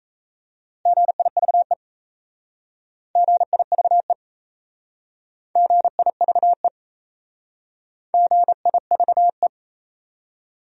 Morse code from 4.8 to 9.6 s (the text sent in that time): GIVE GIVE